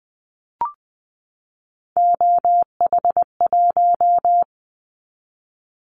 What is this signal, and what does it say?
Morse code, heard as a keyed tone, about 20 wpm, 700 Hz. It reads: OH1
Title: Callsign Prefixes for Top 50 DXCC - 20wpm